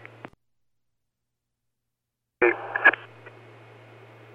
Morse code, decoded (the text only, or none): none